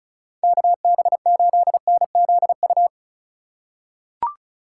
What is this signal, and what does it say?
Morse code, heard as a keyed tone, about 36 wpm, 700 Hz. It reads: KB8NZU
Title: USA Call Signs - 6 Characters - 2x Character Spacing - 35wpm